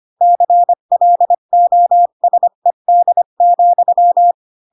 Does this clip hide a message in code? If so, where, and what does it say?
Yes: Morse code from 0.2 to 4.3 s, heard as a keyed tone, about 25 wpm, 700 Hz. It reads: CLOSED,